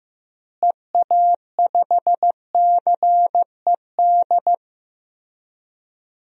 Morse code, heard as a keyed tone, about 15 wpm, 700 Hz. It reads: EA5CED
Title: DXCC-100 Call Signs - 6 Characters - Repeated 3x - 15wpm